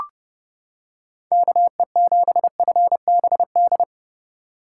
Morse code, heard as a keyed tone, about 30 wpm, 700 Hz. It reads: KE7FBD